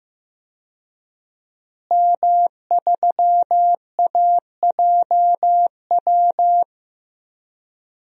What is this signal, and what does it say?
Morse code, heard as a keyed tone, about 15 wpm, 700 Hz. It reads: M3AJW
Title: DXCC-100 Call Signs - 5 Characters - Repeated 2x - 15wpm